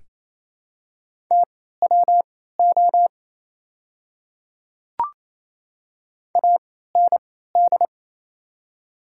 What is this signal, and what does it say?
Morse code, heard as a keyed tone, about 28 wpm, spaced as for 16 wpm, 700 Hz. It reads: TWO AND
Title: 3x Character Spacing - Top 100 Words - 28wpm